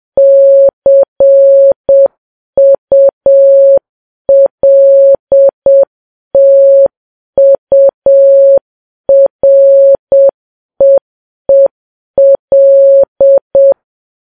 Morse code, heard as a keyed tone, about 7 wpm, 550 Hz. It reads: CULTUREEL
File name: Files Web tls / Morse on9fv / morse7kps